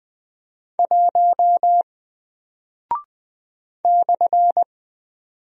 Morse code, heard as a keyed tone, about 20 wpm, 700 Hz. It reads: 1 /
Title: Single Character (Letter:Number:Punctuation:Prosign) 20wpm